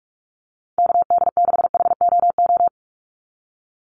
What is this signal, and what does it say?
Morse code, heard as a keyed tone, about 46 wpm, 690 Hz. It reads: KD6HOO
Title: USA Call Signs 45wpm